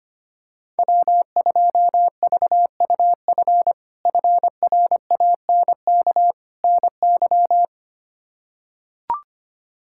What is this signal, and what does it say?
Morse code, heard as a keyed tone, about 25 wpm, 700 Hz. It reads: W2VUF FRANK NY
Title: K1USN SST QSO Practice - 25wpm